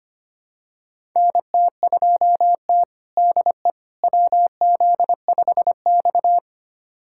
Morse code, heard as a keyed tone, about 25 wpm, 700 Hz. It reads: NT2T DE WZ5X
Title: USA Callsign DE Callsign - 4 Characters - 25wpm